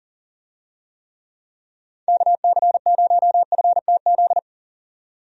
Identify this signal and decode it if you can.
Morse code, heard as a keyed tone, about 40 wpm, 700 Hz. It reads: KC0FTZ
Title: USA Call Signs - 6 Characters - 40wpm